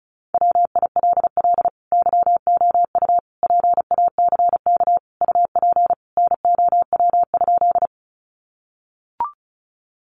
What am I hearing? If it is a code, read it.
Morse code, heard as a keyed tone, about 34 wpm, 700 Hz. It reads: WILL YOU PACK UP NOW?
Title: Sentences from Top 1300 Words 35wpm